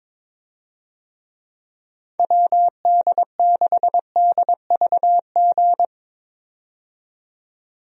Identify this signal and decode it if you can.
Morse code, heard as a keyed tone, about 22 wpm, 700 Hz. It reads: WD6DVG
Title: USA Call Signs - 6 Characters - 22wpm